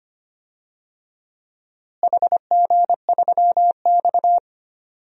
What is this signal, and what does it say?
Morse code, heard as a keyed tone, about 25 wpm, 700 Hz. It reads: HG3X
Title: DXCC-100 Call Signs - 4 Characters - Repeated 2x - 25wpm